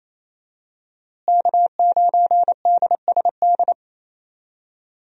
Morse code, heard as a keyed tone, about 27 wpm, 700 Hz. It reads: K9DSD
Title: USA Call Signs - 5 Characters - Repeated 3x - 28wpm